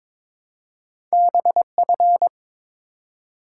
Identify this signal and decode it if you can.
Morse code, heard as a keyed tone, about 22 wpm, 700 Hz. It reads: BF
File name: practice / 22wpm / Two Letter-Number Combinations - 22wpm